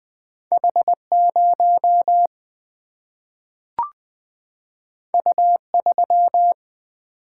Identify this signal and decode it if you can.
Morse code, heard as a keyed tone, about 20 wpm, 700 Hz. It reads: H0 U3
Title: Single-Letter Single-Number 20wpm